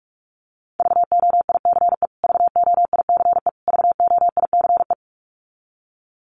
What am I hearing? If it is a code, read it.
Morse code, heard as a keyed tone, about 44 wpm, 700 Hz. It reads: VOICE VOICE VOICE
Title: Top 500 Words Repeated 45wpm